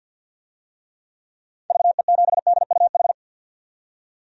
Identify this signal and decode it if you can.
Morse code, heard as a keyed tone, about 46 wpm, 700 Hz. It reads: VE7DUH
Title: DXCC-100 Call Signs - 6 Characters - 50wpm